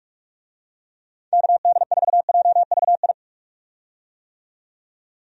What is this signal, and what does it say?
Morse code, heard as a keyed tone, about 46 wpm, 700 Hz. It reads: KD4JVI
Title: USA Call Signs - 6 Characters - 45wpm